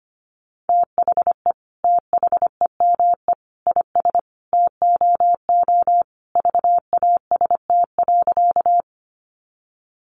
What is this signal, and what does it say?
Morse code, heard as a keyed tone, about 25 wpm, 700 Hz. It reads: THE THEME IS TOO VAST.
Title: Sentences from Top 1300 Words 25wpm